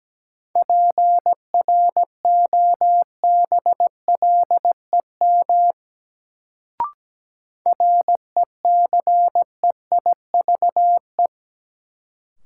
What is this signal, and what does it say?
Morse code, heard as a keyed tone, about 17 wpm, 700 Hz. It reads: PROBLEM RECEIVE
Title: Sentences from Top 1000 Words - New Word Drill - 17wpm